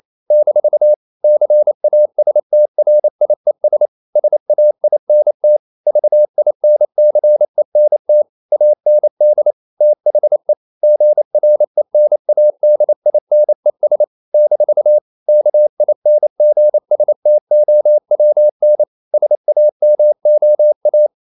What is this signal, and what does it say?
Morse code, heard as a keyed tone, about 28 wpm, 600 Hz. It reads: - CASTRIES SAINT VINCENT AND THE GRENADINES - KINGSTOWN SAMOA
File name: CW / Practice / NationalCapitals28